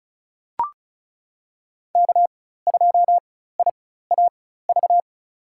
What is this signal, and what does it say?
Morse code, heard as a keyed tone, about 34 wpm, spaced as for 16 wpm, 700 Hz. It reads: K2IAV
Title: USA Call Signs - 5 Characters - 4x Character Spacing - 35wpm